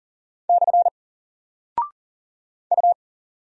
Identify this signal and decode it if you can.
Morse code, heard as a keyed tone, about 40 wpm, 700 Hz. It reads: / U
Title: Single Character (Letter:Number:Punctuation:Prosign) 40wpm